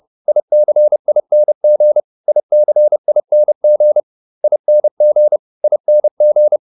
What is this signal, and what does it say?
Morse code, heard as a keyed tone, about 30 wpm, 600 Hz. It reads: ICING ICING ING ING